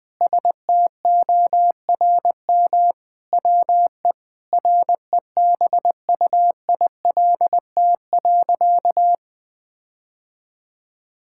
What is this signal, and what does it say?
Morse code, heard as a keyed tone, about 20 wpm, 700 Hz. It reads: STORM WE REBUILT.
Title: Sentences with 5-Word Phrases - 20wpm